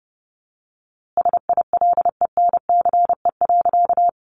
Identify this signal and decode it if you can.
Morse code, heard as a keyed tone, about 30 wpm, 700 Hz. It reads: SILENCE.
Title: A Sound Of Thunder 30wpm